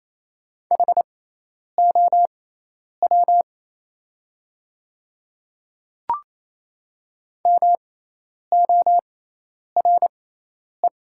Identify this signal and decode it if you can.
Morse code, heard as a keyed tone, about 28 wpm, spaced as for 10 wpm, 700 Hz. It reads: HOW MORE
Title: 6x Character Spacing - Top 200 Words - 28wpm